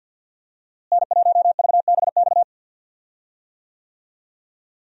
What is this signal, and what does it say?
Morse code, heard as a keyed tone, about 52 wpm, 700 Hz. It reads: N1VBX
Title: USA Call Signs - 5 Characters - Repeated 2x - 50wpm